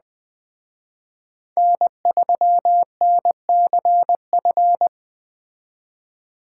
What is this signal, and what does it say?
Morse code, heard as a keyed tone, about 20 wpm, 700 Hz. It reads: N3NCF
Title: Speed Racing - USA 1x3 Call Signs 17wpm